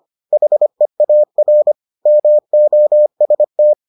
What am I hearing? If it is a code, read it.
Morse code, heard as a keyed tone, about 25 wpm, 600 Hz. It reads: HEAR MOST